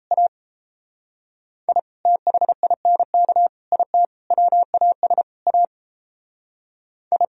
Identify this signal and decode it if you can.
Morse code, heard as a keyed tone, about 34 wpm, 700 Hz. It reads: A I THINK IT WAS A I
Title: Speed-Racing - Sets of 5 Words 28wpm